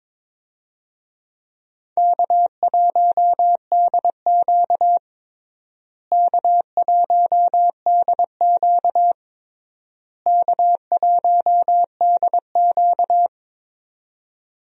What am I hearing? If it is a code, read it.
Morse code, heard as a keyed tone, about 22 wpm, 700 Hz. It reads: K1DQ K1DQ K1DQ
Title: USA Call Signs - 4 Characters - Repeated 3x - 22wpm